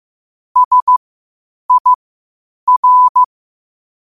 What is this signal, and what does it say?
Morse code, heard as a keyed tone, about 15 wpm, spaced as for 8 wpm, 1000 Hz. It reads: SIR
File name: qotdfiles / 8wpm